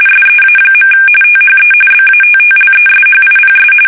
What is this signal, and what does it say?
RTTY, about 45.45 baud. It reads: SEA. THE BATTLESHIPS
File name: fsk850 / usnavy5